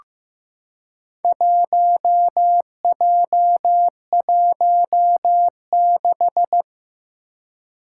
Morse code, heard as a keyed tone, about 15 wpm, 700 Hz. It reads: 1J16
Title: Four Challenging Letter-Number Combinations 15wpm